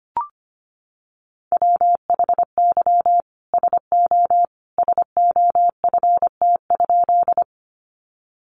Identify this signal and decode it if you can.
Morse code, heard as a keyed tone, about 25 wpm, 700 Hz. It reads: WHY SO SOFT?
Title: Sentences from Top 1500 Words 25wpm